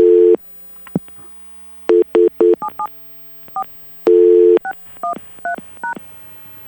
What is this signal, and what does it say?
DTMF, heard as telephone touch tones, tones of about 90 ms. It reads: 77 4 613#